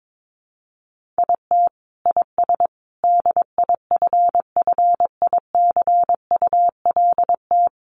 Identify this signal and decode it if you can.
Morse code, heard as a keyed tone, about 22 wpm, 700 Hz. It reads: IT IS DIFFICULT
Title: Sentences from Top 1500 Words 22wpm